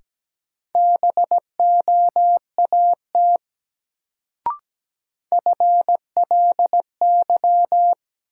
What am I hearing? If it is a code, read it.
Morse code, heard as a keyed tone, about 17 wpm, 700 Hz. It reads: BOAT FLY
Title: Sentences from Top 500 Words - New Word Drill - 17wpm